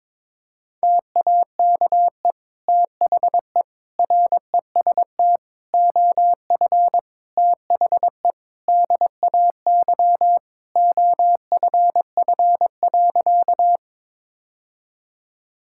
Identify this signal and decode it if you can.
Morse code, heard as a keyed tone, about 21 wpm, 700 Hz. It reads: TAKE THE REST OF THE DAY OFF.
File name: practice / 22wpm / Sentences with 2-Word Phrases - 22wpm